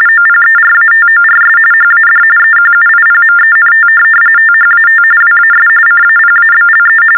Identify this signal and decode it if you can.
TTY/TDD text telephone tones, about 45.45 baud. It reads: OPERATORS HAD EQUIPMENT THAT WAS CAPABLE
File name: teletype-RTTY-45